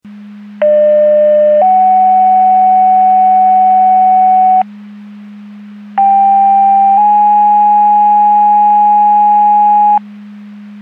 Two-tone page, heard as a one-second tone, then a three-second tone, about 600 then 750 Hz, about 790 then 830 Hz. It is heard from 0.6 to 10.0 s.
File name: Portage 154.385 Paging